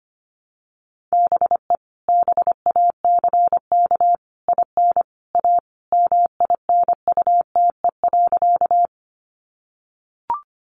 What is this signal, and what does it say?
Morse code, heard as a keyed tone, about 25 wpm, 700 Hz. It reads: BE BACK IN A MINUTE.
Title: Sentences from Top 500 Words 25wpm